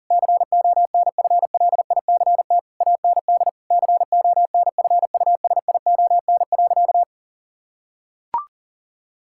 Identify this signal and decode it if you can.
Morse code, heard as a keyed tone, about 40 wpm, 700 Hz. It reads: CONFLICT AND CONFUSION.
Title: Sentences from Top 1600 Words 40wpm